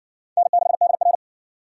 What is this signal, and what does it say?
Morse code, heard as a keyed tone, about 58 wpm, 700 Hz. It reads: N6DR